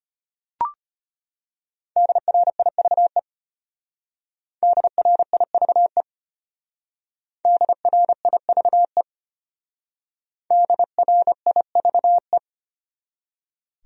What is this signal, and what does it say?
Morse code, heard as a keyed tone, about 32 wpm, 700 Hz. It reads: DRIVE DRIVE DRIVE DRIVE